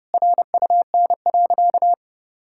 Morse code, heard as a keyed tone, about 30 wpm, 700 Hz. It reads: RUN.
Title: Sentences from Top 1600 Words 30wpm